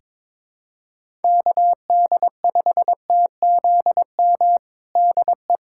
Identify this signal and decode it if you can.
Morse code, heard as a keyed tone, about 22 wpm, 700 Hz. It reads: KD5TZM DE
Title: USA Callsign DE Callsign 22wpm